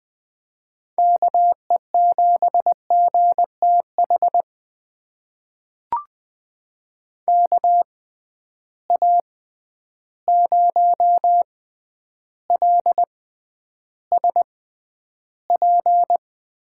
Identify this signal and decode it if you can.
Morse code, heard as a keyed tone, about 20 wpm, spaced as for 10 wpm, 700 Hz. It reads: KE7GTH KA0LSP
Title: USA Call Signs - 6 Characters - 6x Character Spacing - 20wpm